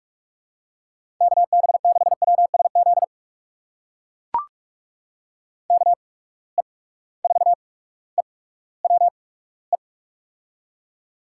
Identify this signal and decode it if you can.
Morse code, heard as a keyed tone, about 44 wpm, spaced as for 19 wpm, 700 Hz. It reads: KB6WSZ KE4EWE
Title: USA Call Signs - 6 Characters - 8x Character Spacing - 45wpm